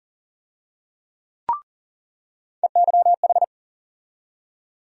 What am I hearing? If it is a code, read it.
Morse code, heard as a keyed tone, about 40 wpm, 700 Hz. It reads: EYH
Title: USA Call Sign Postfixes - 3 Letters 40wpm